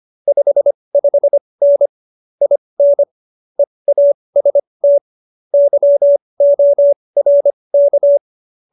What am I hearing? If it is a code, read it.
Morse code, heard as a keyed tone, about 25 wpm, spaced as for 20 wpm, 570 Hz. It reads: 55N IN EAST YORK
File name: QSO220 20